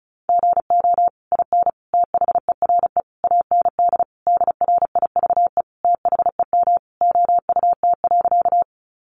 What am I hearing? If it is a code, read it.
Morse code, heard as a keyed tone, about 34 wpm, 700 Hz. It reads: GO IN THERE AND DRIVE THEM OUT.
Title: Sentences from Top 500 Words 35wpm